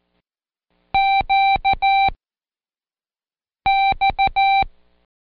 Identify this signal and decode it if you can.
Morse code, heard as a keyed tone, about 14 wpm, spaced as for 5 wpm, 780 Hz. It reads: QX
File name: Code Quick / AudioDownld1 / Seg54-Day6-Pract GQX